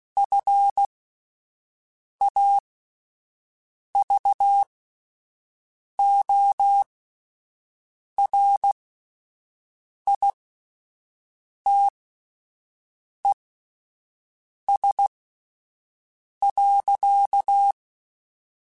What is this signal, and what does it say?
Morse code, heard as a keyed tone, about 16 wpm, spaced as for 5 wpm, 800 Hz. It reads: FAVORITES.